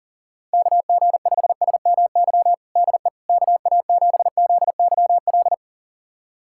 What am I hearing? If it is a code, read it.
Morse code, heard as a keyed tone, about 40 wpm, 700 Hz. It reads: KG5SMY DE KA7ZYL
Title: USA Callsign DE Callsign - 6 Characters - 40wpm